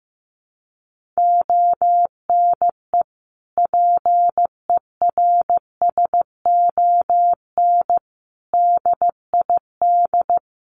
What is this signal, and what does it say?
Morse code, heard as a keyed tone, about 15 wpm, 700 Hz. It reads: ONE PERSON DID
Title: Sentences from Top 1500 Words 15wpm